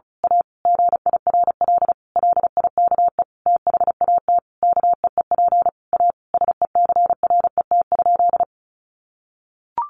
Morse code, heard as a keyed tone, about 34 wpm, 700 Hz. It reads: A GIRL LIKE THAT KEEP A SECRET?
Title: Sentences from Top 1200 Words 35wpm